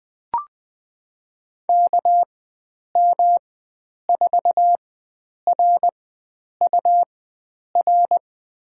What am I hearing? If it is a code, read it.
Morse code, heard as a keyed tone, about 20 wpm, spaced as for 9 wpm, 700 Hz. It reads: KM4RUR
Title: USA Call Signs - 6 Characters - 4x Character Spacing - 20wpm